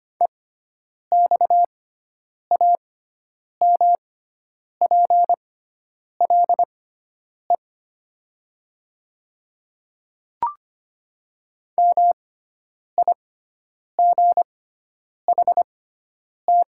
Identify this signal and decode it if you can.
Morse code, heard as a keyed tone, about 25 wpm, spaced as for 9 wpm, 700 Hz. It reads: EXAMPLE MIGHT
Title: 6x Character Spacing - Top 500 Words - 25wpm